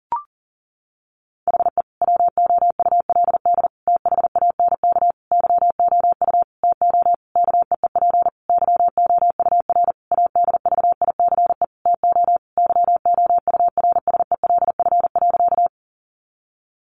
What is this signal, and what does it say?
Morse code, heard as a keyed tone, about 40 wpm, 700 Hz. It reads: HE WOULD THANK YOU TO KEEP YOUR ADVICE TO YOURSELF.
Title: Sentences from Top 1100 Words 40wpm